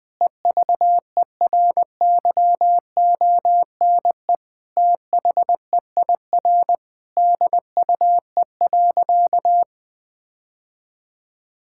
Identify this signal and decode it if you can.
Morse code, heard as a keyed tone, about 20 wpm, 700 Hz. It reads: EVERYONE THEIR DUE.
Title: Sentences from Top 1000 Words - Review Part 6 - 20wpm